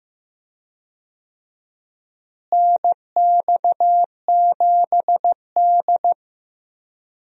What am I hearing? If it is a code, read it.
Morse code, heard as a keyed tone, about 15 wpm, 700 Hz. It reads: NX7D
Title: USA Call Signs - 4 Characters - Repeated 3x - 15wpm